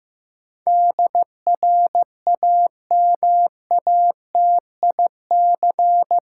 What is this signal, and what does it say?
Morse code, heard as a keyed tone, about 15 wpm, 700 Hz. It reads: DRAMATIC
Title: Sentences from Top 1300 Words - New Word Drill Alt - 15wpm